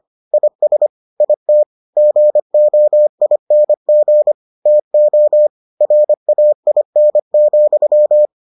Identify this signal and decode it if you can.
Morse code, heard as a keyed tone, about 25 wpm, 600 Hz. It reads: IS IT GOING TO RAIN,